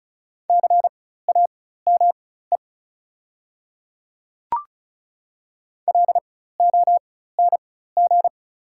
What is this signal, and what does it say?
Morse code, heard as a keyed tone, about 36 wpm, spaced as for 16 wpm, 700 Hz. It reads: CAME LONG
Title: 4x Character Spacing - Top 200 Words - 35wpm